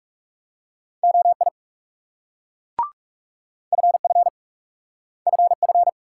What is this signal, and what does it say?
Morse code, heard as a keyed tone, about 38 wpm, 700 Hz. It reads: OI FF FF